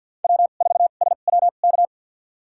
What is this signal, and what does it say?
Morse code, heard as a keyed tone, about 47 wpm, spaced as for 35 wpm, 690 Hz. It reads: W4SWK
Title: USA Call Signs - 2x Character Spacing - 50wpm